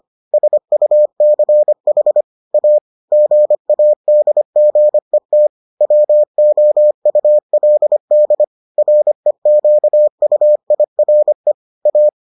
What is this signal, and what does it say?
Morse code, heard as a keyed tone, about 25 wpm, 600 Hz. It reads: SUCH A GADGET WOULD REQUIRE A